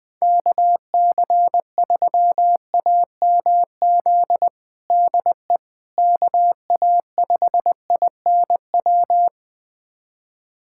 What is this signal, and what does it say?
Morse code, heard as a keyed tone, about 20 wpm, 700 Hz. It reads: KC3AMZ DE KA5INW
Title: USA Callsign DE Callsign - 6 Characters - 20wpm